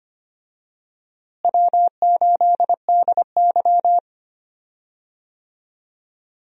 Morse code, heard as a keyed tone, about 25 wpm, 700 Hz. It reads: W8DY
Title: USA Call Signs - 4 Characters - Repeated 2x - 25wpm